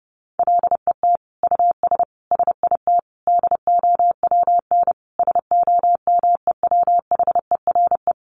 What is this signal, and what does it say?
Morse code, heard as a keyed tone, about 30 wpm, 700 Hz. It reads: LET US SIT DOWN SOMEWHERE
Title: Sentences from Top 1400 Words 30wpm